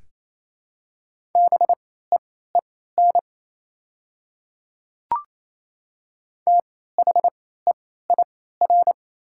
Morse code, heard as a keyed tone, about 28 wpm, spaced as for 16 wpm, 700 Hz. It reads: BEEN THEIR
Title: 3x Character Spacing - Top 200 Words - 28wpm